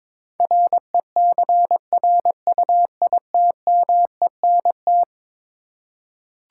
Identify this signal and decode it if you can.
Morse code, heard as a keyed tone, about 22 wpm, 700 Hz. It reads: RECRUITMENT
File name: practice / 22wpm / Eleven Letter Words - 22wpm